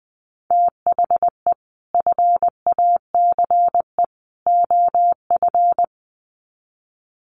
Morse code, heard as a keyed tone, about 20 wpm, 700 Hz. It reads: THE FACE OF
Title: Sets of 3 Words 20wpm